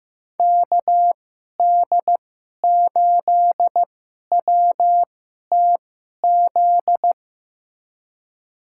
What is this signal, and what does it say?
Morse code, heard as a keyed tone, about 15 wpm, spaced as for 11 wpm, 700 Hz. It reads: KD8WTZ
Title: USA Call Signs - 2x Character Spacing - 15wpm